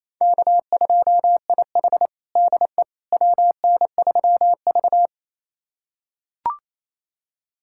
Morse code, heard as a keyed tone, about 28 wpm, 700 Hz. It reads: K2IH DE WN3V
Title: USA Callsign DE Callsign - 4 Characters - 28wpm